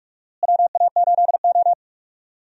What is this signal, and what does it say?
Morse code, heard as a keyed tone, about 43 wpm, 700 Hz. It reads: WA8O
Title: USA Call Signs - 4 Characters - Repeated 3x - 45wpm